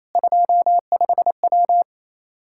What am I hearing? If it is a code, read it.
Morse code, heard as a keyed tone, about 27 wpm, 700 Hz. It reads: 25W
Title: K1USN SST QSO Practice 28wpm